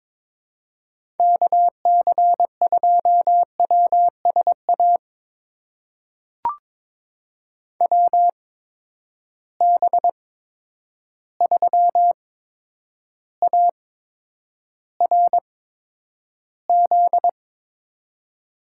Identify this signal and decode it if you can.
Morse code, heard as a keyed tone, about 22 wpm, spaced as for 9 wpm, 700 Hz. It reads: KC2WSA WB3ARZ
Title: USA Call Signs - 6 Characters - 8x Character Spacing - 22wpm